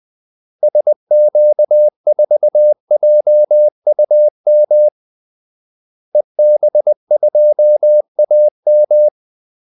Morse code, heard as a keyed tone, about 20 wpm, 600 Hz. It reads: SQ4JUM EB2AM
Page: Kaldesignaler 20 wpm